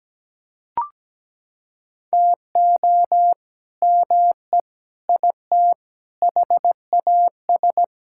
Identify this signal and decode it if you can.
Morse code, heard as a keyed tone, about 17 wpm, 700 Hz. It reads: TO ME IT HAS
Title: Sentences from Top 1000 Words - Review Part 6 - 17wpm